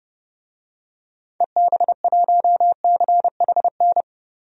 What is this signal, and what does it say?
Morse code, heard as a keyed tone, about 30 wpm, 700 Hz. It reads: EB1CHN
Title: DXCC-100 Call Signs - 6 Characters - 30wpm